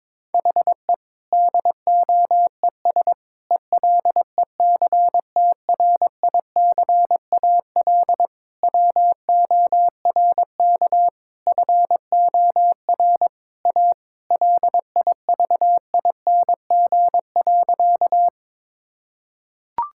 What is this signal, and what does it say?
Morse code, heard as a keyed tone, about 22 wpm, 700 Hz. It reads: HE DOES ELECTRICAL WORK FOR A LIVING.
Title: Sentences from Top 1000 Words - Review Part 6 - 22wpm